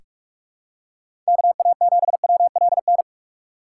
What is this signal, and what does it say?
Morse code, heard as a keyed tone, about 44 wpm, 700 Hz. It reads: KA7WLN
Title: USA Call Signs - 3x Character Spacing - 45wpm